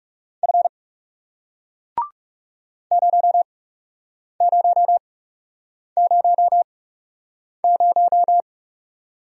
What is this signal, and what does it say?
Morse code, heard as a keyed tone, about 40 wpm, spaced as for 8 wpm, 700 Hz. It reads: F 0000